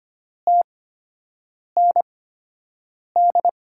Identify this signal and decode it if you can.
Morse code, heard as a keyed tone, about 25 wpm, spaced as for 7 wpm, 700 Hz. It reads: TND
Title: USA Call Signs - 6 Characters - 8x Character Spacing - 25wpm